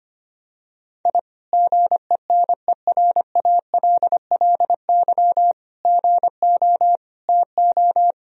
Morse code, heard as a keyed tone, about 25 wpm, 700 Hz. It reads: I GENERALLY GO TO
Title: Sentences from Top 1000 Words - Review Part 4 - 25wpm